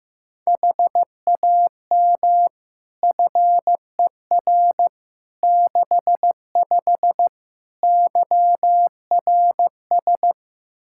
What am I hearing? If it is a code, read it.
Morse code, heard as a keyed tone, about 15 wpm, 700 Hz. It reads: HAM FER 65 YRS
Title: K1USN SST QSO Practice 15wpm